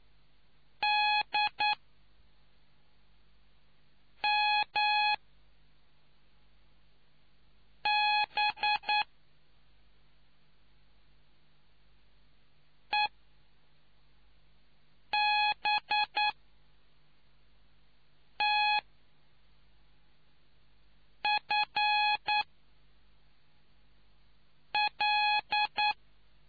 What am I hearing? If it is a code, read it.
Morse code, heard as a keyed tone, about 9 wpm, 840 Hz. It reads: D M B E B T F L